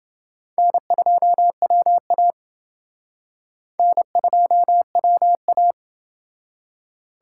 Morse code, heard as a keyed tone, about 28 wpm, 700 Hz. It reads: N2WA N2WA